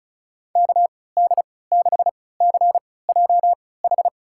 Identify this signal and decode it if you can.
Morse code, heard as a keyed tone, about 36 wpm, spaced as for 20 wpm, 690 Hz. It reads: KD6CJH
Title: USA Call Signs - 6 Characters - 3x Character Spacing - 35wpm